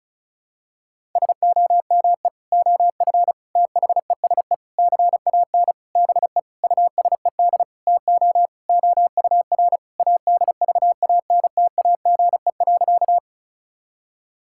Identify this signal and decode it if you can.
Morse code, heard as a keyed tone, about 36 wpm, 700 Hz. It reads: SOME OF THESE CAN BE USED TO OUR ADVANTAGE.
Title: Sentences from Top 1000 Words - Review Part 6 - 35wpm